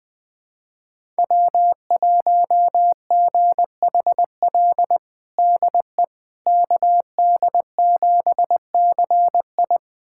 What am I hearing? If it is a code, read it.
Morse code, heard as a keyed tone, about 21 wpm, 700 Hz. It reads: W1GHL DE KD7CI
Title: USA Callsign DE Callsign - 5 Characters - 20wpm